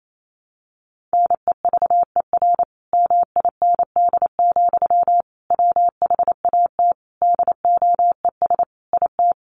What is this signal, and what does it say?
Morse code, heard as a keyed tone, about 28 wpm, 700 Hz. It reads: NEVER MIND, WHAT DOES IT
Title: Sentences from Top 700 Words 28wpm